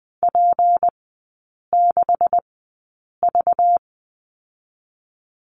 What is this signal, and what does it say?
Morse code, heard as a keyed tone, about 20 wpm, spaced as for 8 wpm, 690 Hz. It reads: P6V
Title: 212 Lesson 34e - Number 8 - Three Characters 20wpm